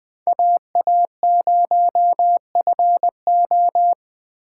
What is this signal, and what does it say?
Morse code, heard as a keyed tone, about 20 wpm, 700 Hz. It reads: AA0FO
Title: Speed Racing - USA 2x2 Call Signs 17wpm